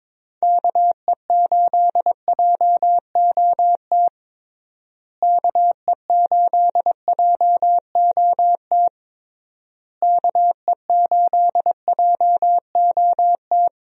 Morse code, heard as a keyed tone, about 22 wpm, 700 Hz. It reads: KE8JOT KE8JOT KE8JOT
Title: USA Call Signs - 6 Characters - Repeated 3x - 22wpm